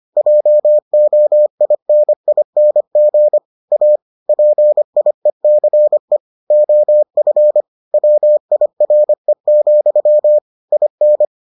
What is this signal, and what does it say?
Morse code, heard as a keyed tone, about 25 wpm, 600 Hz. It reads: JOINING A PIECE OF WIRE, IN